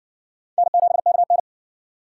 Morse code, heard as a keyed tone, about 56 wpm, 700 Hz. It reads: N7KN